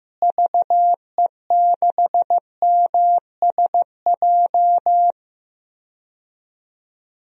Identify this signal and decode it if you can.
Morse code, heard as a keyed tone, about 15 wpm, 700 Hz. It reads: VE6MSJ
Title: DXCC-100 Call Signs - 6 Characters - Repeated 3x - 15wpm